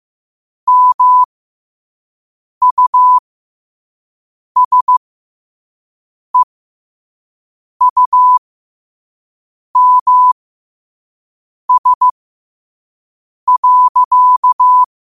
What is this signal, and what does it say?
Morse code, heard as a keyed tone, about 15 wpm, spaced as for 5 wpm, 1000 Hz. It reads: MUSEUMS.
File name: qotdfiles / 5wpm